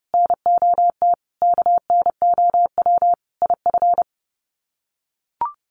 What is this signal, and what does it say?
Morse code, heard as a keyed tone, about 30 wpm, 700 Hz. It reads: NOT KNOW IF
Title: Sets of 5 Words Encore 30wpm